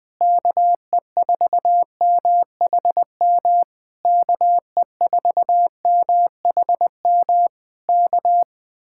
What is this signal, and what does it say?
Morse code, heard as a keyed tone, about 20 wpm, 700 Hz. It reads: KE4MHM KE4MHM K
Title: Speed-Racing Common QSO Phrases - 17wpm